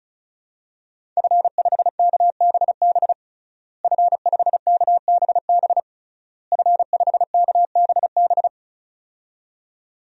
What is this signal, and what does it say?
Morse code, heard as a keyed tone, about 36 wpm, 700 Hz. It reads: F5KBB F5KBB F5KBB
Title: DXCC-100 Call Signs - 5 Characters - Repeated 3x - 35wpm